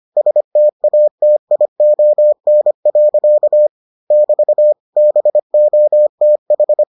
Morse code, heard as a keyed tone, about 25 wpm, 600 Hz. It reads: STATION. <BT> BOTH